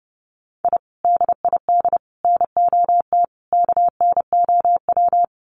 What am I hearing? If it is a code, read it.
Morse code, heard as a keyed tone, about 30 wpm, 700 Hz. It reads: I DID NOT KNOW
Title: Sets of 4 Words 30wpm